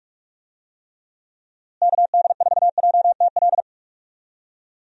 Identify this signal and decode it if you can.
Morse code, heard as a keyed tone, about 46 wpm, 700 Hz. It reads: KD4JTL
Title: USA Call Signs - 6 Characters - Repeated 2x - 45wpm